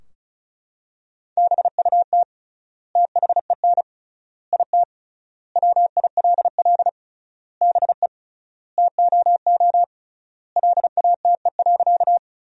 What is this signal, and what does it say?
Morse code, heard as a keyed tone, about 36 wpm, 700 Hz. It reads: BUT THEN IT WILL BE TOO LATE.
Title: 3x Word-Spacing - Sentences from Top 300 Words - 35wpm